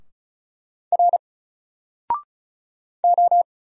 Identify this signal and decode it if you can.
Morse code, heard as a keyed tone, about 33 wpm, 700 Hz. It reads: R O
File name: Single-Letter-Number-35wpm